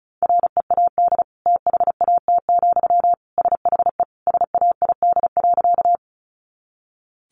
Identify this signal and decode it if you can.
Morse code, heard as a keyed tone, about 34 wpm, 700 Hz. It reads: READ THAT, SHE SAID.
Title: Sentences from Top 200 Words 35wpm